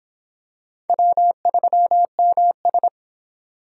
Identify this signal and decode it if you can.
Morse code, heard as a keyed tone, about 25 wpm, 700 Hz. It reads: W3MS